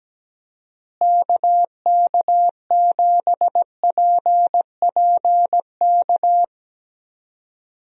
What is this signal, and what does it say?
Morse code, heard as a keyed tone, about 17 wpm, 700 Hz. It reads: KK7PPK
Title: USA Call Signs - 6 Characters - Repeated 2x - 17wpm